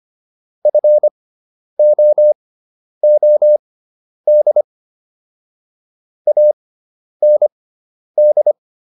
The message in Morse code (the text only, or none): FOOD AND